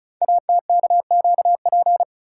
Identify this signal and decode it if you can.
Morse code, heard as a keyed tone, about 36 wpm, 700 Hz. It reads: ATKQP
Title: Code Groups - 5 Letters - 35wpm